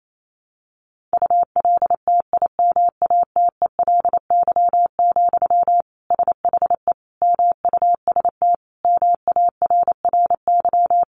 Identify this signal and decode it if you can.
Morse code, heard as a keyed tone, about 28 wpm, 700 Hz. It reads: ULTIMATELY, SHE MUST MARRY
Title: Sentences from Top 1200 Words 28wpm